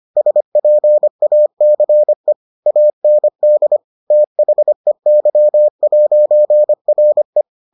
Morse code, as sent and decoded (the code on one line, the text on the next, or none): ... .--. .- -.-. / .- -. -.. / - .... -.-- .----. .-. .
SPACE AND THEY'RE